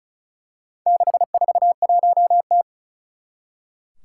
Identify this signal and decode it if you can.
Morse code, heard as a keyed tone, about 36 wpm, 700 Hz. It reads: 641T
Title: Four Challenging Letter-Number Combinations 35wpm